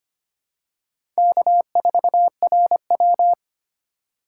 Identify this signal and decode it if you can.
Morse code, heard as a keyed tone, about 25 wpm, 700 Hz. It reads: K4RW
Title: USA Call Signs - 4 Characters - 8x Character Spacing - 25wpm